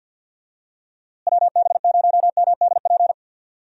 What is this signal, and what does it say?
Morse code, heard as a keyed tone, about 50 wpm, 700 Hz. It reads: WB0MDP
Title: USA Call Signs - 6 Characters - Repeated 3x - 50wpm